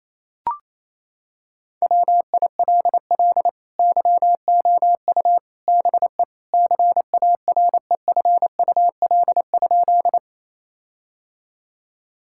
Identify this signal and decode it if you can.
Morse code, heard as a keyed tone, about 28 wpm, 700 Hz. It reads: WILL YOU BE CAREFUL?
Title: Sentences from Top 1600 Words 28wpm